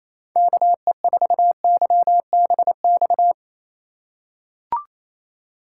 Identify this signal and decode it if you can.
Morse code, heard as a keyed tone, about 28 wpm, 700 Hz. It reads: KE4YBX
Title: USA Call Signs - 6 Characters - 2x Character Spacing - 28wpm